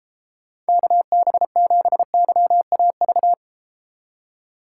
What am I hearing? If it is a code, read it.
Morse code, heard as a keyed tone, about 32 wpm, 700 Hz. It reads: KB7YAV